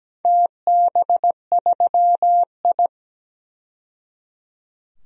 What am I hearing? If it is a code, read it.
Morse code, heard as a keyed tone, about 17 wpm, 700 Hz. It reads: TB3I
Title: DXCC-100 Call Signs - 4 Characters - 17wpm